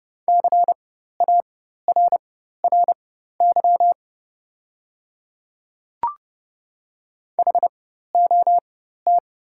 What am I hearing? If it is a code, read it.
Morse code, heard as a keyed tone, about 30 wpm, spaced as for 14 wpm, 690 Hz. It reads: CARRY HOT
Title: 4x Character Spacing - Top 500 Words - 30wpm